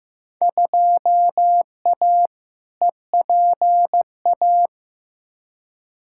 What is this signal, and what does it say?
Morse code, heard as a keyed tone, about 16 wpm, 700 Hz. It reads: 2A EPA
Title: Field Day Exchanges - 15wpm